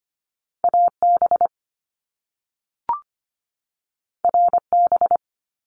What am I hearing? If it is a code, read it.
Morse code, heard as a keyed tone, about 25 wpm, 700 Hz. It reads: AB RB
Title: Top Two Letter Combinations in Words 25wpm